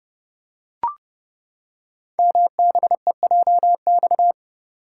Morse code, heard as a keyed tone, about 30 wpm, 690 Hz. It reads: MBEJX
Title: Code Groups - 5 Letters - 30wpm